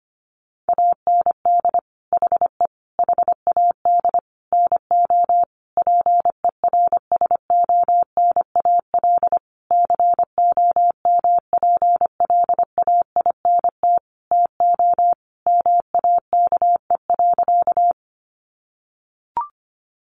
Morse code, heard as a keyed tone, about 25 wpm, 700 Hz. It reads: AND HE HAD NO PERSONAL COMPLAINT TO MAKE.
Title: Sentences from Top 1400 Words 25wpm